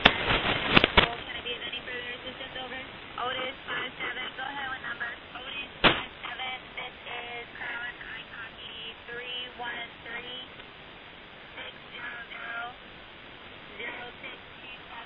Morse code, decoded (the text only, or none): none